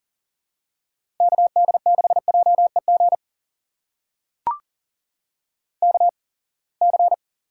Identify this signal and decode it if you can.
Morse code, heard as a keyed tone, about 42 wpm, spaced as for 28 wpm, 700 Hz. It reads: KD6JEG KC
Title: USA Call Signs - 6 Characters - 8x Character Spacing - 40wpm